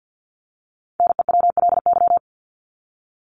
Morse code, heard as a keyed tone, about 50 wpm, 700 Hz. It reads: NEWLY
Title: LY Suffix - Top 100 Words with Base Words 50wpm